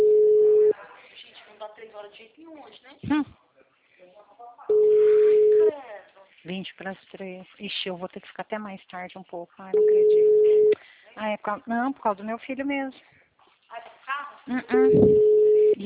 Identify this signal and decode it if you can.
Ringback tone (Europe, 425 Hz), with speech.
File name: out- -2008- - - .59515